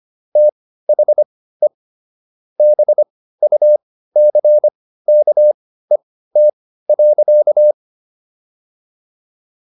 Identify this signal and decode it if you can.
Morse code, heard as a keyed tone, about 25 wpm, spaced as for 15 wpm, 600 Hz. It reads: THE BUCKET.